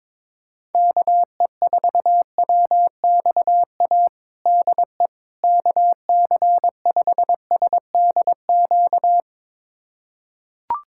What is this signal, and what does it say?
Morse code, heard as a keyed tone, about 22 wpm, 700 Hz. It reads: KE4WXA DE KC5SDQ
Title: USA Callsign DE Callsign - 6 Characters - 22wpm